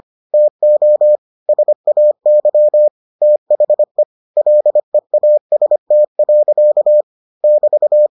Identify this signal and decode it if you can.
Morse code, heard as a keyed tone, about 25 wpm, 600 Hz. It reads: TO SAY THE LEAST. <BT>